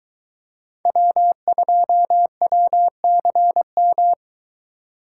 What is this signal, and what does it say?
Morse code, heard as a keyed tone, about 22 wpm, 700 Hz. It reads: W2WCM